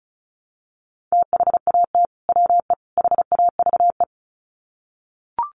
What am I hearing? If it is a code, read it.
Morse code, heard as a keyed tone, about 34 wpm, 700 Hz. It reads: THAT WE HAVE
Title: Sets of 3 Words 35wpm